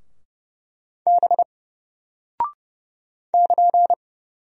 Morse code, heard as a keyed tone, about 30 wpm, 700 Hz. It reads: B <KN>
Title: Single Character (Letter:Number:Punctuation:Prosign) Alt 30wpm